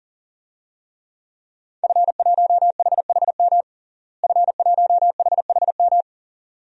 Morse code, heard as a keyed tone, about 40 wpm, 700 Hz. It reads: F1HHM F1HHM
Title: DXCC-100 Call Signs - 5 Characters - Repeated 2x - 40wpm